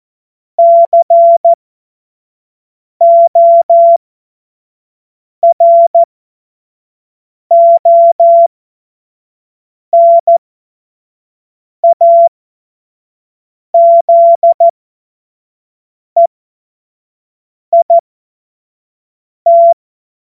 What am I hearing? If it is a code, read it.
Morse code, heard as a keyed tone, about 14 wpm, spaced as for 5 wpm, 680 Hz. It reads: CORONAZEIT
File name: Morsekurs / Witze / Witz 31. August 2021, 25 F70